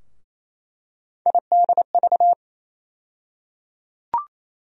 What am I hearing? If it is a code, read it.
Morse code, heard as a keyed tone, about 28 wpm, 700 Hz. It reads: IDV
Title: USA Call Sign Postfixes - 3 Letters 28wpm